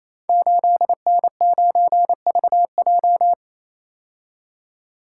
Morse code, heard as a keyed tone, about 27 wpm, 700 Hz. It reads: 8N9VJ
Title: Code Groups - 5 Letter:Numbers - 28wpm